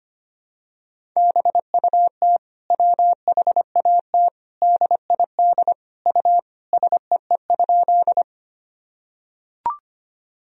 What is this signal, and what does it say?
Morse code, heard as a keyed tone, about 25 wpm, 700 Hz. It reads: BUT WHAT DID U SEE?
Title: Sentences from Top 100 Words and Abbreviations 25wpm